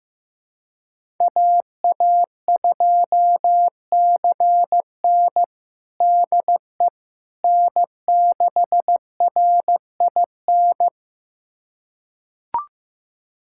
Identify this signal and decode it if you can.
Morse code, heard as a keyed tone, about 16 wpm, 700 Hz. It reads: AA2CN DE N6RIN
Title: USA Callsign DE Callsign - 5 Characters - 15wpm